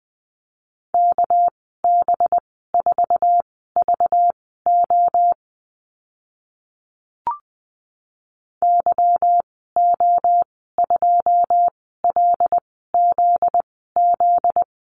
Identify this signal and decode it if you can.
Morse code, heard as a keyed tone, about 20 wpm, spaced as for 14 wpm, 700 Hz. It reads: KB4VO YO2LZZ
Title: 245 Lesson 39g - Letter z - Call Signs 20wpm